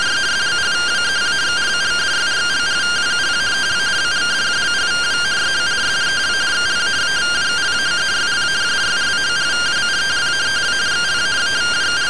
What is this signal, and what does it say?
Two-tone teleprinter signal, about 170 Hz shift.
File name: findings / CT SAGITAL 27.gen-19 rtty rx